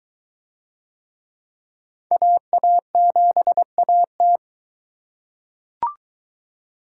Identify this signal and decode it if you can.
Morse code, heard as a keyed tone, about 23 wpm, 700 Hz. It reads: AA7AT